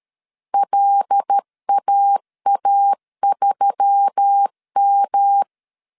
Morse code, heard as a keyed tone, about 13 wpm, 800 Hz. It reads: LAA3M
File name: CW / DET / 1 6 1500C S A 12wpm 3440